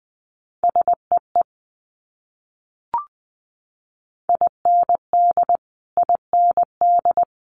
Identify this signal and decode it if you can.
Morse code, heard as a keyed tone, about 20 wpm, 700 Hz. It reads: SEE IND IND
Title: Multi-letter Phonemes Repeated 3x 20wpm